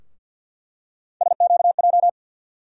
Morse code, heard as a keyed tone, about 49 wpm, 700 Hz. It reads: SQJ
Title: USA Call Sign Postfixes - 3 Letters 50wpm